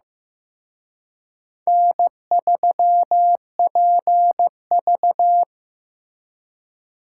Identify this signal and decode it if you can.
Morse code, heard as a keyed tone, about 18 wpm, 700 Hz. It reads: N3PV
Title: Speed Racing - USA Call Signs 15wpm